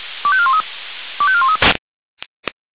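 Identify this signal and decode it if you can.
Five-tone selective call (ZVEI), heard as quick runs of five tones, tones of about 70 ms. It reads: 25413 25413
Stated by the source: ZVEI five-tone selective call received on an SDR.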